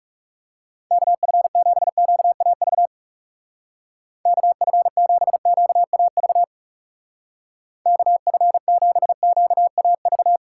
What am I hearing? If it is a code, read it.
Morse code, heard as a keyed tone, about 38 wpm, 700 Hz. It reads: KF7QAV KF7QAV KF7QAV